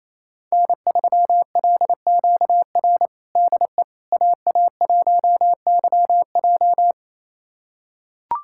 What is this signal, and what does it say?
Morse code, heard as a keyed tone, about 27 wpm, 700 Hz. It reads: N3LQR DE AA1YJ
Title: USA Callsign DE Callsign - 5 Characters - 28wpm